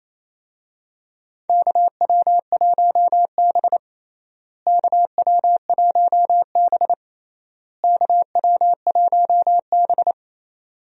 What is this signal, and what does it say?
Morse code, heard as a keyed tone, about 28 wpm, 700 Hz. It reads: KW1B KW1B KW1B
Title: USA Call Signs - 4 Characters - Repeated 3x - 28wpm